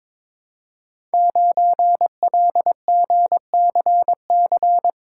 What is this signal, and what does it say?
Morse code, heard as a keyed tone, about 22 wpm, 700 Hz. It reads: 9LGCC
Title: Code Groups - 5 Letter:Numbers - 22wpm